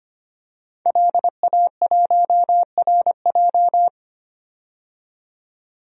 Morse code, heard as a keyed tone, about 25 wpm, 700 Hz. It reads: LA1RJ
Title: DXCC-100 Call Signs - 5 Characters - 25wpm